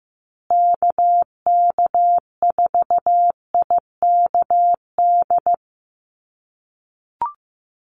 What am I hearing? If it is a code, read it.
Morse code, heard as a keyed tone, about 15 wpm, 700 Hz. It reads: KK4IKD
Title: USA Call Signs Repeated 2x 15wpm